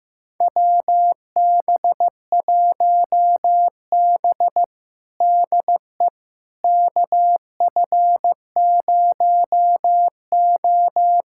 Morse code, heard as a keyed tone, about 15 wpm, 700 Hz. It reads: WB1B DE KF0O
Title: USA Callsign DE Callsign - 4 Characters - 15wpm